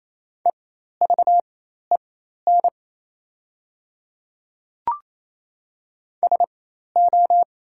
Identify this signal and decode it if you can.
Morse code, heard as a keyed tone, about 28 wpm, spaced as for 13 wpm, 690 Hz. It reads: EVEN SO
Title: 4x Character Spacing - Top 200 Words - 28wpm